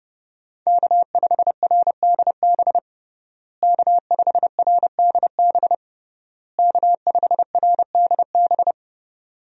Morse code, heard as a keyed tone, about 30 wpm, 700 Hz. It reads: K5RDB K5RDB K5RDB
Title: USA Call Signs - 5 Characters - Repeated 3x - 30wpm